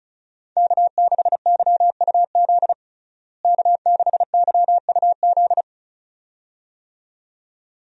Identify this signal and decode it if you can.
Morse code, heard as a keyed tone, about 36 wpm, 700 Hz. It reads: K6YUZ K6YUZ
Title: USA Call Signs - 5 Characters - Repeated 2x - 35wpm